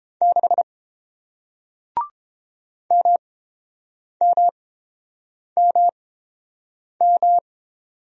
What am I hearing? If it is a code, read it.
Morse code, heard as a keyed tone, about 29 wpm, spaced as for 7 wpm, 700 Hz. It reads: 6 MMMM